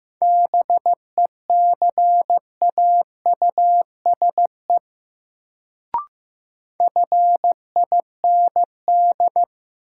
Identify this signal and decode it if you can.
Morse code, heard as a keyed tone, about 15 wpm, 700 Hz. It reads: BECAUSE FIND
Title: Sets of 2 Words - New Word Drill - 15wpm